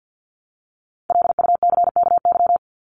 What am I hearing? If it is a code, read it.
Morse code, heard as a keyed tone, about 50 wpm, 700 Hz. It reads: LUCKY
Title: Top 1751-2000 Words 50wpm